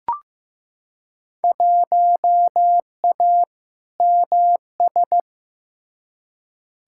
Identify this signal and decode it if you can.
Morse code, heard as a keyed tone, about 15 wpm, 700 Hz. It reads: 1A MS
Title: Field Day Exchanges - 15wpm